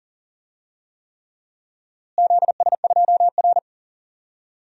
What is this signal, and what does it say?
Morse code, heard as a keyed tone, about 40 wpm, 700 Hz. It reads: ZS2R
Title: DXCC-100 Call Signs - 4 Characters - Repeated 3x - 40wpm